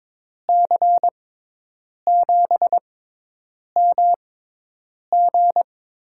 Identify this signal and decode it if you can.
Morse code, heard as a keyed tone, about 21 wpm, spaced as for 8 wpm, 700 Hz. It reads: C7MG
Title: USA Call Signs - 5 Characters - 6x Character Spacing - 22wpm